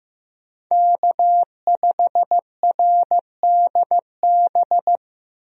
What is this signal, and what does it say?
Morse code, heard as a keyed tone, about 15 wpm, 700 Hz. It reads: K5RDB
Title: USA Call Signs - 5 Characters - Repeated 3x - 15wpm